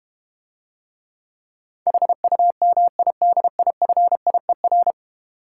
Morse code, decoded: HUMIDIFIER

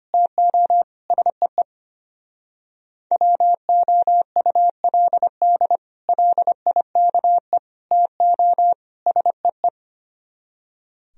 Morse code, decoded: TO SEE WOULD LIKE TO SEE

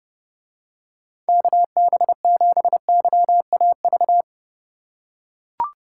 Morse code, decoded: KB7YAV

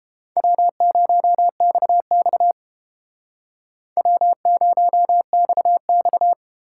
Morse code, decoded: W0XX W0XX

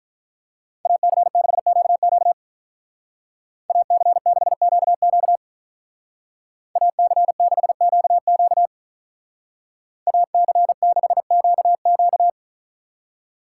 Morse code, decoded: AC6QQ AC6QQ AC6QQ AC6QQ

